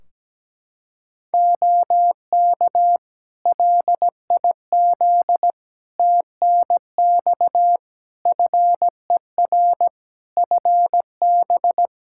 Morse code, heard as a keyed tone, about 17 wpm, 700 Hz. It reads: OK LIZ TNX FER FB
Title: Practice QSO - Exchange 4 - Ending - 17wpm